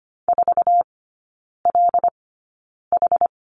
Morse code, heard as a keyed tone, about 25 wpm, spaced as for 9 wpm, 690 Hz. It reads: 4LH
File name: Lesson 14e - Number 2 - Three Characters 25wpm